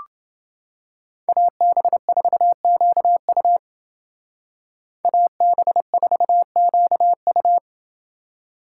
Morse code, heard as a keyed tone, about 29 wpm, 700 Hz. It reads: AB4QU AB4QU